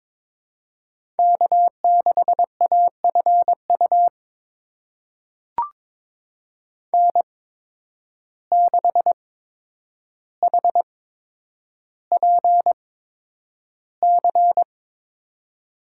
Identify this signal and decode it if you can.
Morse code, heard as a keyed tone, about 22 wpm, spaced as for 9 wpm, 700 Hz. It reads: K6AFU N6HPC
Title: USA Call Signs - 8x Character Spacing - 22wpm